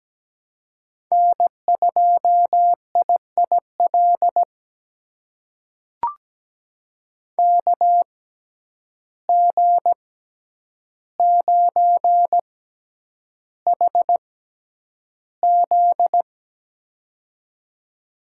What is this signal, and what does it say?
Morse code, heard as a keyed tone, about 17 wpm, spaced as for 9 wpm, 700 Hz. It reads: N2IIL KG9HZ
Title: USA Call Signs - 5 Characters - 6x Character Spacing - 17wpm